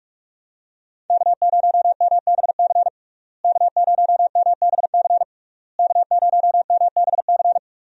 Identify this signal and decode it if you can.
Morse code, heard as a keyed tone, about 46 wpm, 700 Hz. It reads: K0MBC K0MBC K0MBC
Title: USA Call Signs - 5 Characters - Repeated 3x - 45wpm